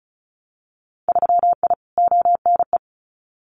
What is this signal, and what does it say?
Morse code, heard as a keyed tone, about 35 wpm, 700 Hz. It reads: 3I ONE
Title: Winter Field Day Exchanges Encore 35wpm